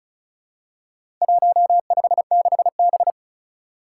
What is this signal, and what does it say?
Morse code, heard as a keyed tone, about 34 wpm, 700 Hz. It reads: 156B
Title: Four Challenging Letter-Number Combinations 35wpm